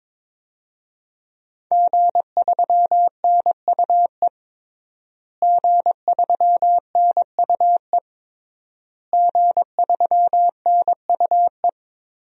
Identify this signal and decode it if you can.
Morse code, heard as a keyed tone, about 21 wpm, 700 Hz. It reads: G3NUE G3NUE G3NUE
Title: DXCC-100 Call Signs - 5 Characters - Repeated 3x - 22wpm